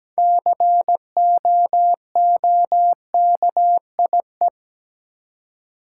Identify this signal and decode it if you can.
Morse code, heard as a keyed tone, about 17 wpm, 700 Hz. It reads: COOKIE
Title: Six Letter Words - 17wpm